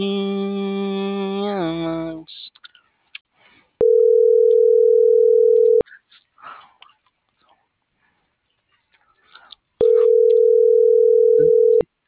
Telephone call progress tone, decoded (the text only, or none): ringback tone